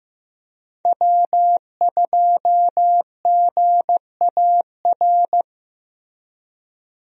Morse code, heard as a keyed tone, about 15 wpm, 700 Hz. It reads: W2GAR
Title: USA Call Signs - 5 Characters - Repeated 2x - 15wpm